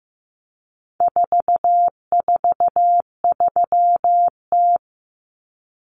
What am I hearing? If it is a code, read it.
Morse code, heard as a keyed tone, about 15 wpm, 700 Hz. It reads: 443T
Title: Common Cut Numbers - 4-Digit 15wpm